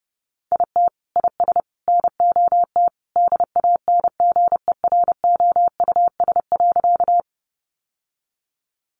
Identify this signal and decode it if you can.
Morse code, heard as a keyed tone, about 30 wpm, 700 Hz. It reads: IT IS NOT DANGEROUS.
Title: Sentences from Top 1200 Words 30wpm